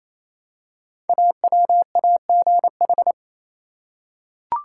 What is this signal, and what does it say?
Morse code, heard as a keyed tone, about 27 wpm, 700 Hz. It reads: AWAGH
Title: Code Groups - 5 Letter:Numbers - 28wpm